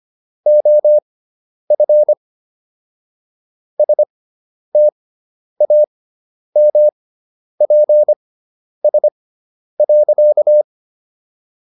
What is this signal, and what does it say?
Morse code, heard as a keyed tone, about 25 wpm, spaced as for 10 wpm, 600 Hz. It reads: OF STAMPS.